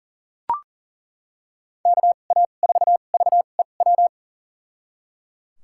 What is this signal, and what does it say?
Morse code, heard as a keyed tone, about 38 wpm, spaced as for 28 wpm, 700 Hz. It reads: KA4VEW
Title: USA Call Signs - 6 Characters - 2x Character Spacing - 40wpm